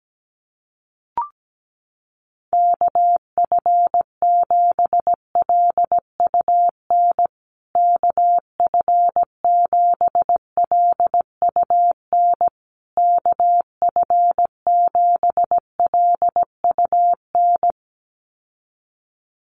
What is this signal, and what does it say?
Morse code, heard as a keyed tone, about 17 wpm, 700 Hz. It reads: KF7LUN KF7LUN KF7LUN
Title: USA Call Signs Repeated 3x 17wpm